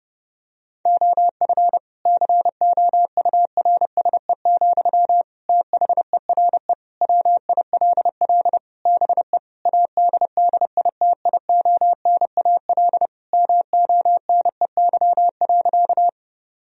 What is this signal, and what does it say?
Morse code, heard as a keyed tone, about 30 wpm, 700 Hz. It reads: OF COURSE, THERE WILL BE ADDITIONAL MONEY.
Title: Sentences from Top 1000 Words - Review Part 4 - 30wpm